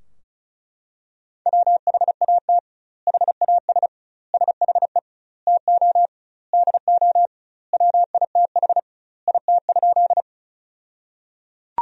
Morse code, heard as a keyed tone, about 35 wpm, 700 Hz. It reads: WHAT HAS SHE TO DO WITH IT?
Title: 2x Word-Spacing - Sentences from Top 100 Words - 35wpm